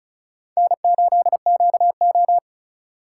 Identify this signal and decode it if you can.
Morse code, heard as a keyed tone, about 34 wpm, 700 Hz. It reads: N8QO
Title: USA Call Signs - 4x Character Spacing - 35wpm